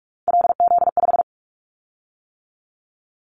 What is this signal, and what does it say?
Morse code, heard as a keyed tone, about 46 wpm, 700 Hz. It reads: LZ5
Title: Callsign Prefixes for Top 50 DXCC - 45wpm